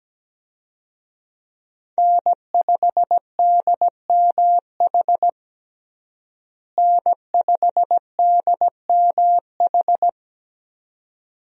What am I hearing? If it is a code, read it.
Morse code, heard as a keyed tone, about 17 wpm, 700 Hz. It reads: N5DMH N5DMH
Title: USA Call Signs - 5 Characters - Repeated 2x - 17wpm